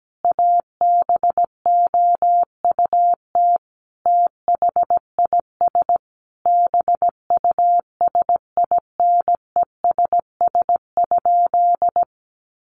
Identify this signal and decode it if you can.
Morse code, heard as a keyed tone, about 17 wpm, 700 Hz. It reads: ABOUT THIS BUSINESS?
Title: Sentences from Top 600 Words 17wpm